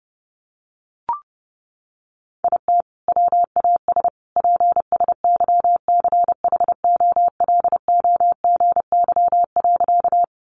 Morse code, heard as a keyed tone, about 30 wpm, 700 Hz. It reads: IT WAS PSYCHOLOGY.
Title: Sentences from Top 1400 Words 30wpm